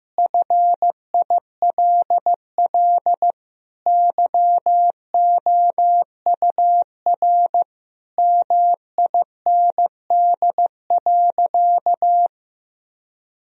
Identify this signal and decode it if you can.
Morse code, heard as a keyed tone, about 15 wpm, 700 Hz. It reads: FILL YOUR MIND.
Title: Sentences from Top 1000 Words - Review Part 3 - 15wpm